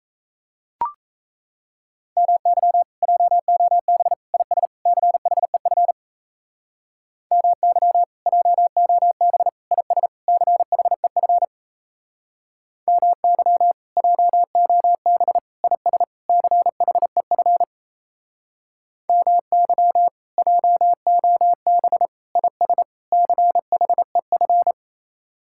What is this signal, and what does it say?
Morse code, heard as a keyed tone, about 36 wpm, 700 Hz. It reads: MY JOB IS CHEF MY JOB IS CHEF MY JOB IS CHEF MY JOB IS CHEF